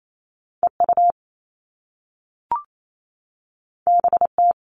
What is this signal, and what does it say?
Morse code, heard as a keyed tone, about 28 wpm, 700 Hz. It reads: EU BT
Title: All Cut Numbers - 2-Digit 28wpm